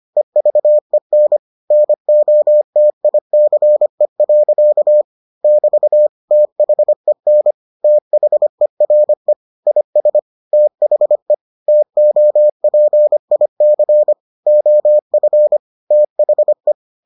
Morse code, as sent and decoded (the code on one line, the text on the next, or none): . ...- . -. / -. --- - .. -.-. .-.-.- / -...- / - .... -. / - .... .-. / .. ... / - .... / - --- .--. .. -.-. / --- ..-. / - .... .
EVEN NOTICE. <BT> THEN THERE IS THE TOPIC OF THE